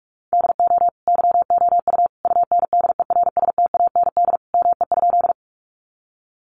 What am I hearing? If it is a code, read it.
Morse code, heard as a keyed tone, about 44 wpm, 700 Hz. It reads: DO YOU UNDERSTAND ME?
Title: Sentences from Top 600 Words 45wpm